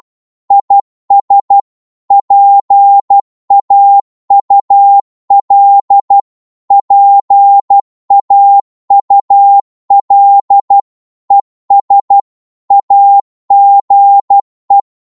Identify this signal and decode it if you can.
Morse code, heard as a keyed tone, about 12 wpm, 800 Hz. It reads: IS PAUL PAUL ES AGE
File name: Morse / qso-12wpm